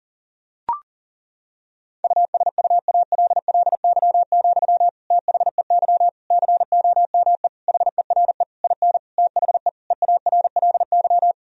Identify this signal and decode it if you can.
Morse code, heard as a keyed tone, about 40 wpm, 700 Hz. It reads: USUALLY, THEY COME HERE IN THE EARLY
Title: Sentences from Top 1000 Words - Review Part 3 - 40wpm